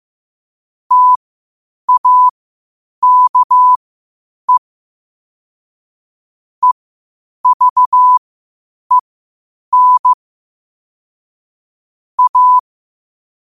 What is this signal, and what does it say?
Morse code, heard as a keyed tone, about 15 wpm, spaced as for 8 wpm, 1000 Hz. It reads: TAKE EVEN A
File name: qotdfiles / 8wpm